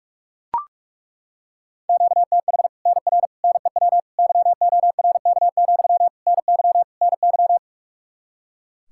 Morse code, heard as a keyed tone, about 42 wpm, 700 Hz. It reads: QTH NR NEW YORK, NY NY
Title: Practice QSO - Exchange 1 - Beginning - 45wpm